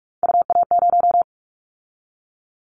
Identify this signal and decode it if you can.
Morse code, heard as a keyed tone, about 46 wpm, 700 Hz. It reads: UA0
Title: Callsign Prefixes for Top 50 DXCC - 45wpm